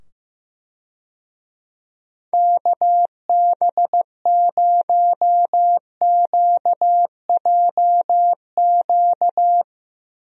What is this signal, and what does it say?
Morse code, heard as a keyed tone, about 15 wpm, 700 Hz. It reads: KB0QJQ
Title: USA Call Signs - 6 Characters - 15wpm